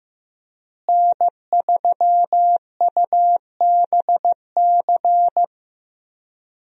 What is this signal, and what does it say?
Morse code, heard as a keyed tone, about 15 wpm, 700 Hz. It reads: N3UBC
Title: USA Call Signs - 5 Characters - Repeated 3x - 15wpm